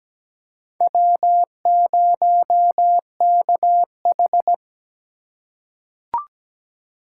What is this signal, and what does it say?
Morse code, heard as a keyed tone, about 17 wpm, 700 Hz. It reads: W0KH
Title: USA Call Signs - 8x Character Spacing - 17wpm